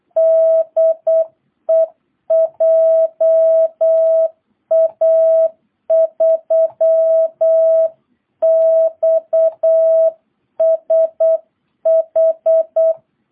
Morse code, heard as a keyed tone, about 8 wpm, 660 Hz. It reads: DE JA3XSH